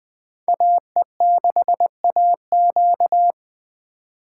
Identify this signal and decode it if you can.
Morse code, heard as a keyed tone, about 25 wpm, 700 Hz. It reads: AE6AQ
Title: Speed Racing - USA 2x2 Call Signs 20wpm